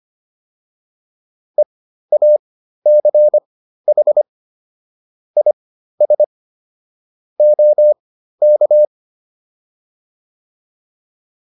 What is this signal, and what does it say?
Morse code, heard as a keyed tone, about 25 wpm, spaced as for 13 wpm, 600 Hz. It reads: EACH IS OK